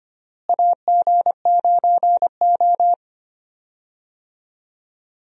Morse code, heard as a keyed tone, about 25 wpm, 700 Hz. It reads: AG9O
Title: USA Call Signs - 4 Characters - Repeated 2x - 25wpm